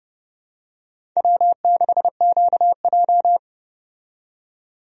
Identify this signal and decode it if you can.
Morse code, heard as a keyed tone, about 30 wpm, 700 Hz. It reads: W6QJ
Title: USA Call Signs - 4 Characters - 30wpm